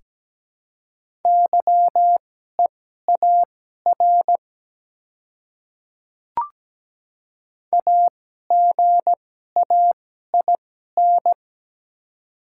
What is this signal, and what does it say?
Morse code, heard as a keyed tone, about 17 wpm, spaced as for 12 wpm, 700 Hz. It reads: YEAR AGAIN
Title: 2x Character Spacing - Top 200 Words - 17wpm